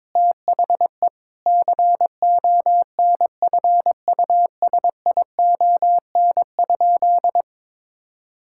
Morse code, heard as a keyed tone, about 22 wpm, 700 Hz. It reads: THE CONFUSION?
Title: Sentences from Top 1600 Words 22wpm